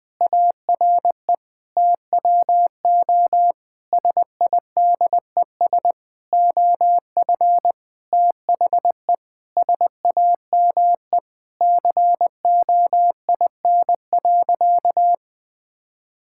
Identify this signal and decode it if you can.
Morse code, heard as a keyed tone, about 20 wpm, 700 Hz. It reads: ARE TWO SIDES OF THE SAME COIN.
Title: Sentences with 7-Word Phrases - 20wpm